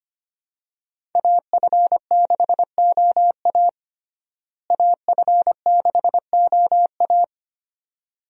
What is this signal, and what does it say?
Morse code, heard as a keyed tone, about 25 wpm, 700 Hz. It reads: AF6OA AF6OA
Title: USA Call Signs - 5 Characters - Repeated 2x - 25wpm